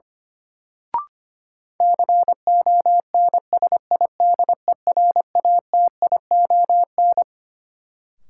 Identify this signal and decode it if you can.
Morse code, heard as a keyed tone, about 25 wpm, 700 Hz. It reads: CONSIDERATION
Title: Sentences from Top 1500 Words - New Word Drill Alt - 25wpm